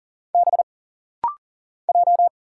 Morse code, heard as a keyed tone, about 40 wpm, 700 Hz. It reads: B J
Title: Single Letters - Rapid-Fire 40wpm